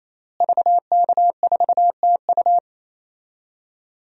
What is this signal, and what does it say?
Morse code, heard as a keyed tone, about 28 wpm, 700 Hz. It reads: VK4TU
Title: DXCC-100 Call Signs - 5 Characters - Repeated 3x - 28wpm